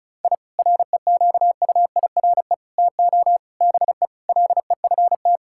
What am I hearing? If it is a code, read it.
Morse code, heard as a keyed tone, about 34 wpm, 700 Hz. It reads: I REQUIRE TO BE LEFT
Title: Sentences from Top 1000 Words - Review Part 6 - 35wpm